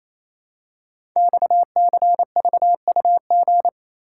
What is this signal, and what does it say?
Morse code, heard as a keyed tone, about 29 wpm, 690 Hz. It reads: XCVUG
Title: Code Groups - 5 Letter:Numbers - 28wpm